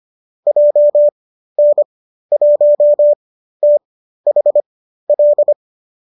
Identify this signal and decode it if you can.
Morse code, heard as a keyed tone, about 25 wpm, spaced as for 13 wpm, 590 Hz. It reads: JN1THL